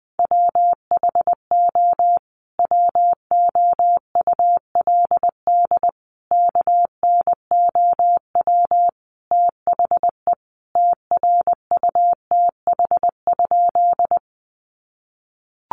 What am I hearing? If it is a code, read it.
Morse code, heard as a keyed tone, about 20 wpm, 700 Hz. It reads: WHO WOULD KNOW THE TRUTH?
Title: Sentences from Top 800 Words 20wpm